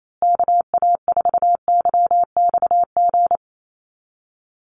Morse code, heard as a keyed tone, about 28 wpm, 700 Hz. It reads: KA4YXG
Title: USA Call Signs Repeated 2x 28wpm